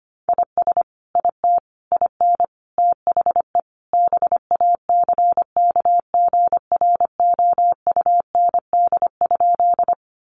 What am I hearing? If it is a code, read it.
Morse code, heard as a keyed tone, about 25 wpm, 700 Hz. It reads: IS IT IN THE BACKGROUND?
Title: Sentences from Top 1300 Words 25wpm